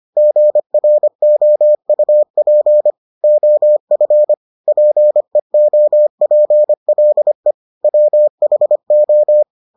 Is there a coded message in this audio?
Yes: Morse code from 0.2 to 9.4 s, heard as a keyed tone, about 25 wpm, 600 Hz. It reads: GROUP OF PEOPLE WHO